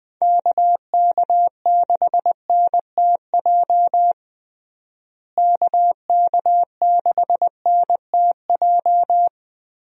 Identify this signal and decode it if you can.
Morse code, heard as a keyed tone, about 20 wpm, 700 Hz. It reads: KK6NTJ KK6NTJ
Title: USA Call Signs - 6 Characters - Repeated 3x - 20wpm